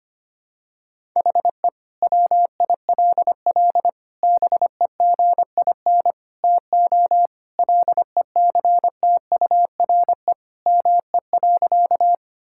Morse code, heard as a keyed tone, about 25 wpm, 700 Hz. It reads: HE WILL BEGIN TO LECTURE ME.
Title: Sentences from Top 1600 Words 25wpm